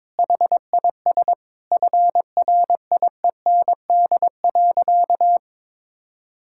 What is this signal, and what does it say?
Morse code, heard as a keyed tone, about 21 wpm, 700 Hz. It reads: HIS FRIEND.
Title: Sentences from Top 1600 Words 22wpm